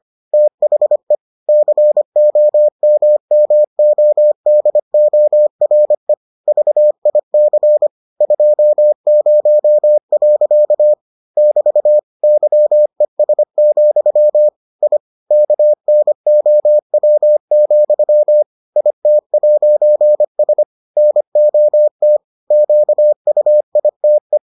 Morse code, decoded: THE COMMODORE VIC 20. <BT> YES, I KNOW, IT'S NOT QUITE

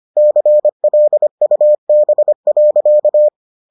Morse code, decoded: CLUB.

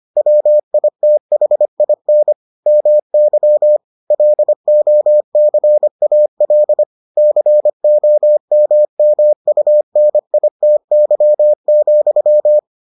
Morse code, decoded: WITHIN MY LOCAL COMMUNITY,

